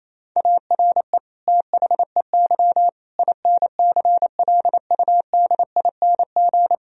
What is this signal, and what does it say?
Morse code, heard as a keyed tone, about 28 wpm, 700 Hz. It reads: ARE THEY INCLUDING
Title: Sentences from Top 1000 Words - Review Part 4 - 28wpm